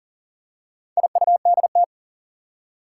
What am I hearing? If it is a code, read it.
Morse code, heard as a keyed tone, about 40 wpm, 700 Hz. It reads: IUDT
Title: Four Challenging Letter-Number Combinations 40wpm